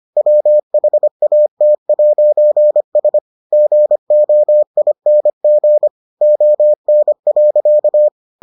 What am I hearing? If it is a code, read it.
Morse code, heard as a keyed tone, about 25 wpm, 600 Hz. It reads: WHAT'S GOING ON.